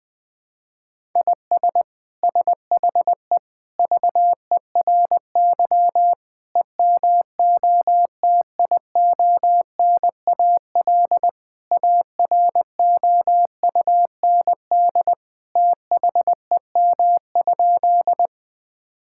Morse code, heard as a keyed tone, about 20 wpm, 700 Hz. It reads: IS SHE VERY EMOTIONAL AROUND THEM?
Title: Sentences from Top 1000 Words - Review Part 5 - 20wpm